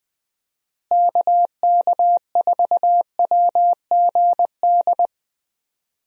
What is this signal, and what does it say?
Morse code, heard as a keyed tone, about 20 wpm, 700 Hz. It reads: KK4WGD
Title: USA Call Signs - 6 Characters - Repeated 3x - 20wpm